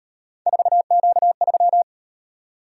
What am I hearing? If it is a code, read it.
Morse code, heard as a keyed tone, about 36 wpm, 700 Hz. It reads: 4Q3